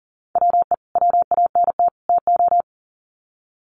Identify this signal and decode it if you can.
Morse code, heard as a keyed tone, about 40 wpm, 700 Hz. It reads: WE WANT TO
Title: Sets of 3 Words 40wpm